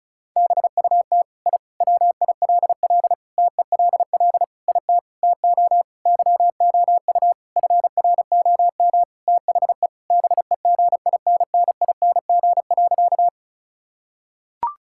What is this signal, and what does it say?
Morse code, heard as a keyed tone, about 34 wpm, 700 Hz. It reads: BUT I WILL TELL IT TO YOU FROM THE BEGINNING.
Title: Sentences from Top 1000 Words - Review Part 5 - 35wpm